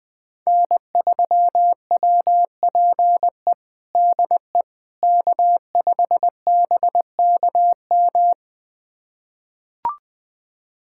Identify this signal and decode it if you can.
Morse code, heard as a keyed tone, about 20 wpm, 700 Hz. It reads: N3WPE DE K5BKM
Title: USA Callsign DE Callsign - 5 Characters - 20wpm